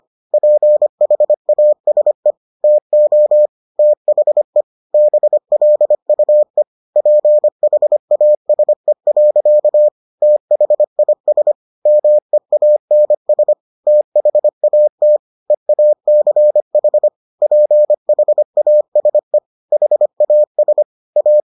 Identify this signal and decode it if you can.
Morse code, heard as a keyed tone, about 25 wpm, 600 Hz. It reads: PHASE TO THE BLUE PHASE. THIS MEANS THAT EACH PHASE HAS A